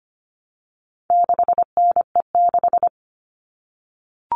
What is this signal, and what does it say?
Morse code, heard as a keyed tone, about 25 wpm, 700 Hz. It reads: 6NE6
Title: All Cut Numbers - 4-Digit 25wpm